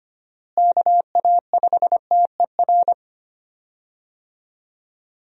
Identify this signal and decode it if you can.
Morse code, heard as a keyed tone, about 25 wpm, 700 Hz. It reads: KA5TER
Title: USA Call Signs - 6 Characters - Repeated 3x - 25wpm